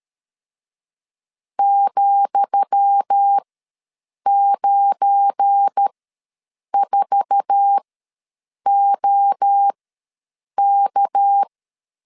Morse code, heard as a keyed tone, about 13 wpm, spaced as for 7 wpm, 800 Hz. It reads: ,94OK
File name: LEZIONI / 10 SIGNS / 10D COMA ws 12wpm 3618